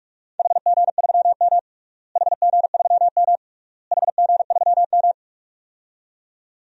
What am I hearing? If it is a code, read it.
Morse code, heard as a keyed tone, about 44 wpm, 700 Hz. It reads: HG3M HG3M HG3M
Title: DXCC-100 Call Signs - 4 Characters - Repeated 3x - 45wpm